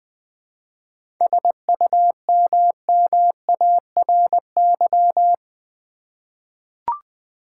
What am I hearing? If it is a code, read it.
Morse code, heard as a keyed tone, about 20 wpm, 700 Hz. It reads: SUMMARY
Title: 7 Letter Words - First 3 Letters Given - 20wpm